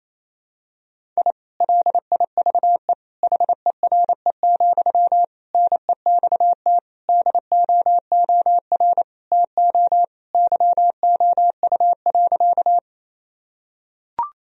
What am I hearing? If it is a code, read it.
Morse code, heard as a keyed tone, about 28 wpm, 700 Hz. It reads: I LIVE HERE, NEXT DOOR TO YOU.
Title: Sentences from Top 1000 Words - Review Part 1 - 28wpm